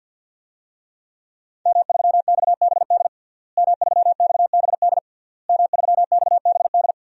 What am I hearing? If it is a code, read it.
Morse code, heard as a keyed tone, about 51 wpm, 700 Hz. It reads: M3XBD M3XBD M3XBD
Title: DXCC-100 Call Signs - 5 Characters - Repeated 3x - 50wpm